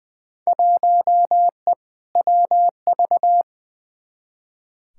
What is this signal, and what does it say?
Morse code, heard as a keyed tone, about 20 wpm, 700 Hz. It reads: 1E WV
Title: Field Day Exchanges Encore - 20wpm